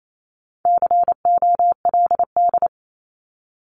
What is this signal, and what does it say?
Morse code, heard as a keyed tone, about 28 wpm, 700 Hz. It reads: COLD
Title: Top 500 Words Repeated 28wpm